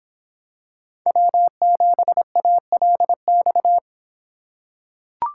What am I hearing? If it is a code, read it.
Morse code, heard as a keyed tone, about 26 wpm, 700 Hz. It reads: W7ALX